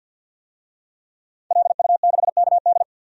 Morse code, heard as a keyed tone, about 52 wpm, 700 Hz. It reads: LU6KD
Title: DXCC-100 Call Signs - 5 Characters - Repeated 2x - 50wpm